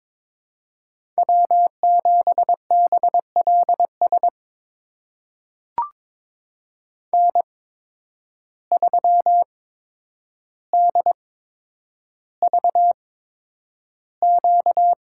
Morse code, heard as a keyed tone, about 21 wpm, spaced as for 9 wpm, 700 Hz. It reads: W7BLS N3DVQ
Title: USA Call Signs - 5 Characters - 8x Character Spacing - 22wpm